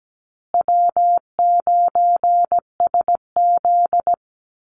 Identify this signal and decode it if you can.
Morse code, heard as a keyed tone, about 17 wpm, 700 Hz. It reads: W9SZ
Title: Easy Call Signs Repeated 3x - 17wpm